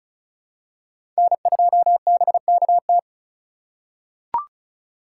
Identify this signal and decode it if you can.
Morse code, heard as a keyed tone, about 36 wpm, 700 Hz. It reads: N2BKT
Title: USA Call Signs - 5 Characters - 3x Character Spacing - 35wpm